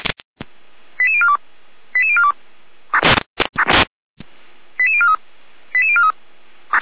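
Five-tone selective call (ZVEI), heard as quick runs of five tones, tones of about 70 ms. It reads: 88042 88042 88043 88043
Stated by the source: ZVEI five-tone selective call received on an SDR.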